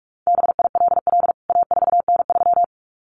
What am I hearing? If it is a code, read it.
Morse code, heard as a keyed tone, about 44 wpm, 700 Hz. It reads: BILL A4N3
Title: CWT Exchange 45wpm